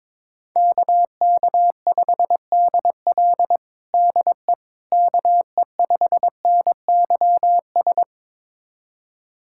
Morse code, heard as a keyed tone, about 22 wpm, 700 Hz. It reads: KK5DL DE KE5NYS
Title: USA Callsign DE Callsign 22wpm